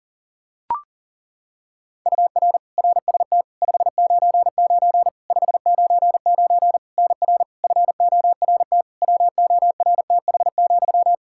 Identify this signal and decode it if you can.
Morse code, heard as a keyed tone, about 40 wpm, 700 Hz. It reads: UR RST 599 599 NR FORT WORTH,
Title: Practice QSO - Exchange 1 - Beginning - 40wpm